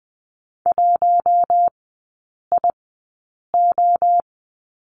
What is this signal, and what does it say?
Morse code, heard as a keyed tone, about 20 wpm, spaced as for 8 wpm, 700 Hz. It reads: 1IO
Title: 138 Lesson 23e - Letter f - Three Characters 20wpm